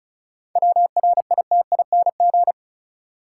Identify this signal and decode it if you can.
Morse code, heard as a keyed tone, about 36 wpm, 700 Hz. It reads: WRITING
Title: Sentences from Top 800 Words - New Word Drill - 35wpm